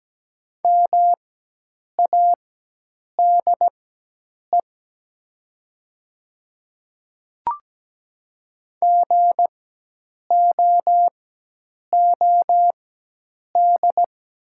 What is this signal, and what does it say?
Morse code, heard as a keyed tone, about 17 wpm, spaced as for 8 wpm, 700 Hz. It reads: MADE GOOD
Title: 4x Character Spacing - Top 200 Words - 17wpm